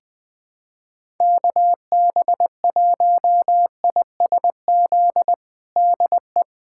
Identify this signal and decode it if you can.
Morse code, heard as a keyed tone, about 20 wpm, 700 Hz. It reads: KB1ISZ DE
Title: USA Callsign DE Callsign 20wpm